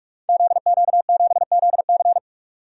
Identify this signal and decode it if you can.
Morse code, heard as a keyed tone, about 44 wpm, 700 Hz. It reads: ZQ7ZC
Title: Code Groups - 5 Letter:Numbers - 45wpm